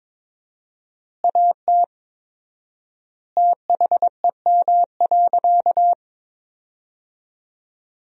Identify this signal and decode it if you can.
Morse code, heard as a keyed tone, about 22 wpm, 700 Hz. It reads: AT THEM.
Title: 4x Word-Spacing - Sentences from Top 100 Words - 22wpm